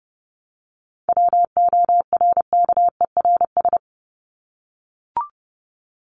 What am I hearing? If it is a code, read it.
Morse code, heard as a keyed tone, about 30 wpm, 700 Hz. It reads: WORKERS
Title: ES:S Suffix - Top 100 Words with Base Words 30wpm